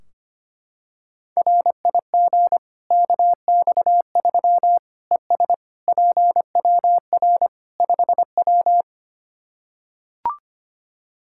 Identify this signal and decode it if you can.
Morse code, heard as a keyed tone, about 25 wpm, 700 Hz. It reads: RIG KX3 ES PWR 5W
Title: Practice QSO - Exchange 2 - Info - 25wpm